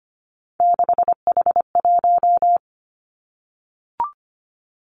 Morse code, heard as a keyed tone, about 25 wpm, 700 Hz. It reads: 6H1
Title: Three Challenging Letter-Number Combinations 25wpm